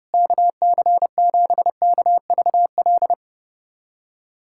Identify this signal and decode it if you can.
Morse code, heard as a keyed tone, about 32 wpm, 700 Hz. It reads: KC7KVL
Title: USA Callsign DE Callsign - 6 Characters - 30wpm